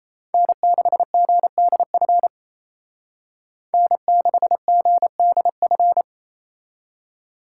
Morse code, heard as a keyed tone, about 35 wpm, 700 Hz. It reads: N6GDF N6GDF